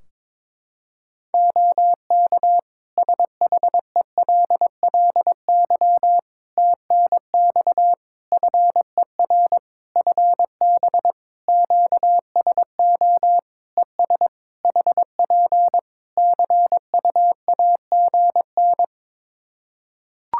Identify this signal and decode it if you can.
Morse code, heard as a keyed tone, about 22 wpm, 700 Hz. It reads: OK SHELLY TNX FER FB QSO ES HP CUAGN
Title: Practice QSO - Exchange 4 - Ending - 22wpm